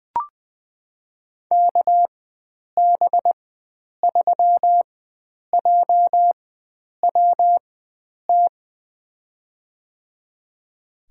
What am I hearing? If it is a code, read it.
Morse code, heard as a keyed tone, about 20 wpm, spaced as for 9 wpm, 700 Hz. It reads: KB3JWT
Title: USA Call Signs - 4x Character Spacing - 20wpm